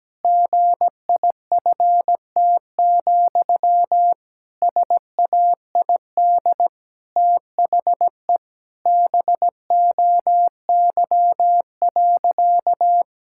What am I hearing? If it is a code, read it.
Morse code, heard as a keyed tone, about 17 wpm, 700 Hz. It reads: GIFT, SAID THE BOY.
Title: Sentences from Top 1000 Words - Review Part 6 - 17wpm